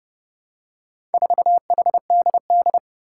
Morse code, heard as a keyed tone, about 30 wpm, 700 Hz. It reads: 4HDD
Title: Four Challenging Letter-Number Combinations 30wpm